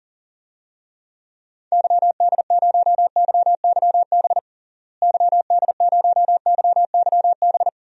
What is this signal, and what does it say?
Morse code, heard as a keyed tone, about 40 wpm, 700 Hz. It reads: YD0YYB YD0YYB
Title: DXCC-100 Call Signs - 6 Characters - Repeated 3x - 40wpm